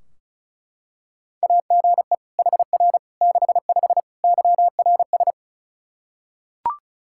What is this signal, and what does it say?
Morse code, heard as a keyed tone, about 36 wpm, 700 Hz. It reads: AGE HR 65 YRS
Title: Practice QSO - Exchange 3 - More Info - 35wpm